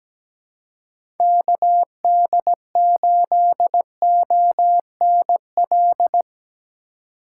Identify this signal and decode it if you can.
Morse code, heard as a keyed tone, about 17 wpm, 700 Hz. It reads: KD8ONL
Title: USA Call Signs - 4x Character Spacing - 17wpm